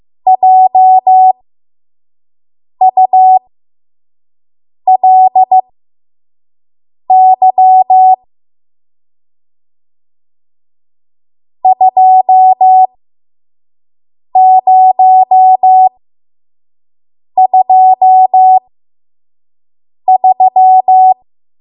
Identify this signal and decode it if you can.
Morse code, heard as a keyed tone, about 15 wpm, spaced as for 5 wpm, 750 Hz. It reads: JULY 2023
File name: Morse / 05WPM